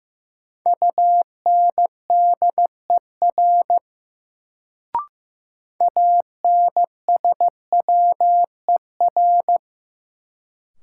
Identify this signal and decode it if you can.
Morse code, heard as a keyed tone, about 15 wpm, 700 Hz. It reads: UNDER ANSWER
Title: Sentences from Top 300 Words - New Word Drill - 15wpm